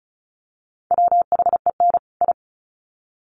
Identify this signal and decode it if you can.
Morse code, heard as a keyed tone, about 34 wpm, 700 Hz. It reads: WHEN I
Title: Sets of 2 Words 35wpm